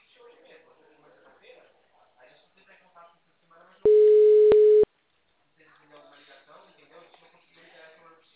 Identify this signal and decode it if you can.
Ringback tone (Europe, 425 Hz), with speech.
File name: out- -2014- - - .35193